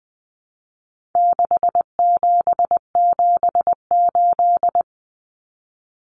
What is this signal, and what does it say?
Morse code, heard as a keyed tone, about 20 wpm, 700 Hz. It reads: 6778
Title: Numbers - 4-Digit 20wpm